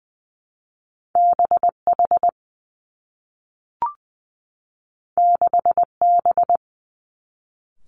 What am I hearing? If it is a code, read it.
Morse code, heard as a keyed tone, about 20 wpm, 700 Hz. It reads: B H 6 B
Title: Two Challenging Letter-Number Combinations 20wpm